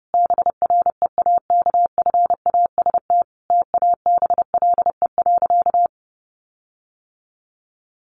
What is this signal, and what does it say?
Morse code, heard as a keyed tone, about 30 wpm, 700 Hz. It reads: BREAKFAST TABLE.
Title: Sentences from Top 1500 Words 30wpm